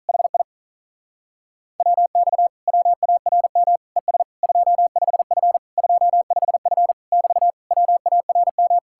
Morse code, heard as a keyed tone, about 48 wpm, 700 Hz. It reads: HI WX WARM ES 25F 25F <BT> WARM